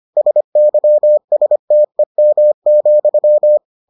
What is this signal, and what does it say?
Morse code, heard as a keyed tone, about 25 wpm, 600 Hz. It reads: SYSTEM,